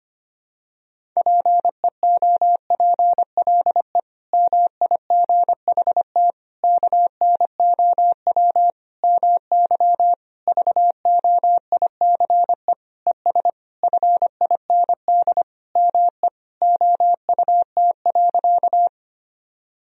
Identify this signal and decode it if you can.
Morse code, heard as a keyed tone, about 25 wpm, 700 Hz. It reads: PEOPLE MIGHT KNOW MY VOICE ES FIND ME OUT.
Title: Sentences from Top 500 Words and Abbreviations 25wpm